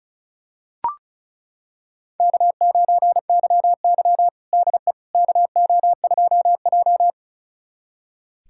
Morse code, heard as a keyed tone, about 36 wpm, 700 Hz. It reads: K9YY DE KO2J
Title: USA Callsign DE Callsign - 4 Characters - 35wpm